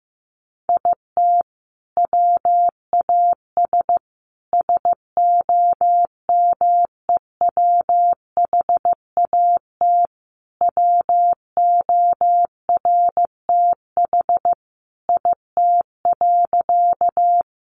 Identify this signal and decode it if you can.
Morse code, heard as a keyed tone, about 14 wpm, 700 Hz. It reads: IT WAS SOMEWHAT WORTH IT.
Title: Sentences from Top 1300 Words 15wpm